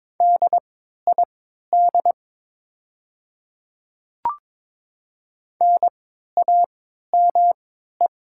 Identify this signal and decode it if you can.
Morse code, heard as a keyed tone, about 22 wpm, spaced as for 12 wpm, 690 Hz. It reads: DID NAME
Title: 3x Character Spacing - Top 300 Words - 22wpm